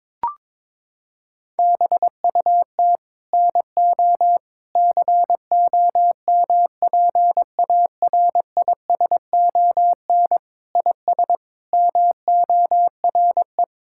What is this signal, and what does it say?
Morse code, heard as a keyed tone, about 22 wpm, 700 Hz. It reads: BUT NO COMPARISON IS MORE
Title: Sentences from Top 1600 Words 22wpm